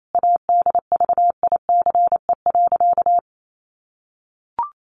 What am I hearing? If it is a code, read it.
Morse code, heard as a keyed tone, about 28 wpm, 700 Hz. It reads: ADVICE.
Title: Sentences from Top 1100 Words 28wpm